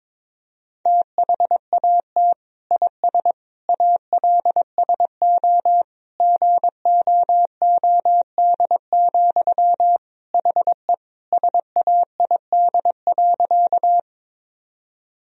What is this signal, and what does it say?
Morse code, heard as a keyed tone, about 22 wpm, 700 Hz. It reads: THAT IS ALSO GOOD, HE SAID.
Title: Sentences from Top 1000 Words - Review Part 1 - 22wpm